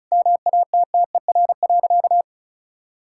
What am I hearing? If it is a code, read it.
Morse code, heard as a keyed tone, about 36 wpm, 700 Hz. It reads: MATTER.
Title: Sentences from Top 1000 Words - Review Part 5 - 35wpm